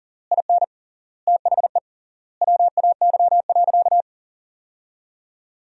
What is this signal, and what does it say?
Morse code, heard as a keyed tone, about 40 wpm, 700 Hz. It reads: IN THE WAY.
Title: 3x Word-Spacing - Sentences from Top 400 Words - 40wpm